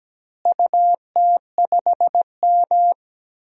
Morse code, heard as a keyed tone, about 17 wpm, 700 Hz. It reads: UT5M
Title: DXCC-100 Call Signs - 4 Characters - Repeated 3x - 17wpm